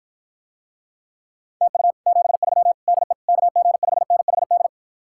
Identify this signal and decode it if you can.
Morse code, heard as a keyed tone, about 53 wpm, 700 Hz. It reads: TU 73 DE KG5NHD